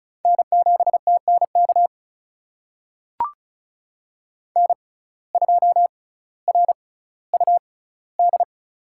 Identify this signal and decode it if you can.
Morse code, heard as a keyed tone, about 36 wpm, spaced as for 18 wpm, 700 Hz. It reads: N7TNK N2RUD
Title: USA Call Signs - 5 Characters - 6x Character Spacing - 35wpm